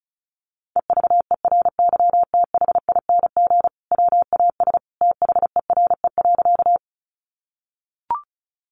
Morse code, heard as a keyed tone, about 34 wpm, 700 Hz. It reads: EVERYTHING WAS THERE.
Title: Sentences from Top 700 Words 35wpm